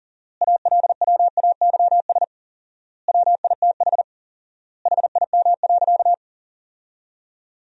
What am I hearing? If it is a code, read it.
Morse code, heard as a keyed tone, about 40 wpm, 700 Hz. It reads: ALWAYS WITH HIM.
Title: 4x Word-Spacing - Sentences from Top 400 Words - 40wpm